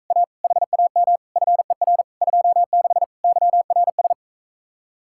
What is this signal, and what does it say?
Morse code, heard as a keyed tone, about 45 wpm, 700 Hz. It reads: A HAM FER 26 YRS